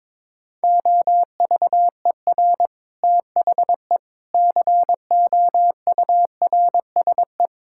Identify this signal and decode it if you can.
Morse code, heard as a keyed tone, about 21 wpm, 700 Hz. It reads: OVER THE COURSE
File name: Sentences with 7-Word Phrases - 22wpm